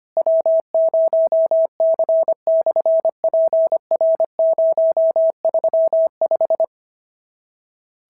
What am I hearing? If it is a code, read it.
Morse code, heard as a keyed tone, about 25 wpm, 650 Hz. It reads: W0C/PR035
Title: SOTA QSO Practice 25wpm